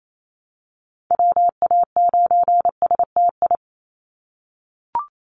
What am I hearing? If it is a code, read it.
Morse code, heard as a keyed tone, about 28 wpm, 700 Hz. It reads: WA9STI
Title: NA SOTA Activator Callsigns 28wpm